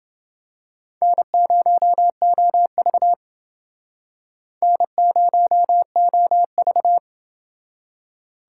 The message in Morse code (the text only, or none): N0OV N0OV